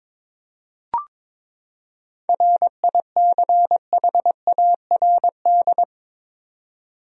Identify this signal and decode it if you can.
Morse code, heard as a keyed tone, about 22 wpm, 700 Hz. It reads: RICHARD
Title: USA Common Names - 22wpm